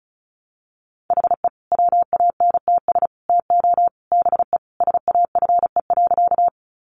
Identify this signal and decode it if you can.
Morse code, heard as a keyed tone, about 34 wpm, 700 Hz. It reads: HE WANTS TO BE SAFE.
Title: Additional Easy Phrases 35wpm